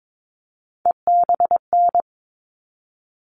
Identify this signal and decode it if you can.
Morse code, heard as a keyed tone, about 22 wpm, 700 Hz. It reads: EBN
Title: All Cut Numbers - 3-Digit 22wpm